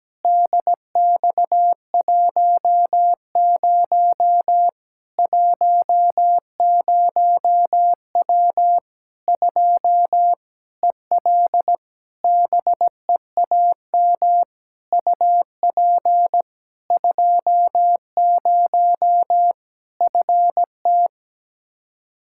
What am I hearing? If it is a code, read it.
Morse code, heard as a keyed tone, about 17 wpm, 700 Hz. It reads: DX10 10W 2 EL BEAM UP 20 FT
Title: Practice QSO - Exchange 2 - Info - 17wpm